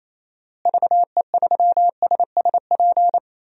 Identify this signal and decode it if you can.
Morse code, heard as a keyed tone, about 28 wpm, 700 Hz. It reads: VE3SSP
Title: DXCC-100 Call Signs - 6 Characters - Repeated 3x - 28wpm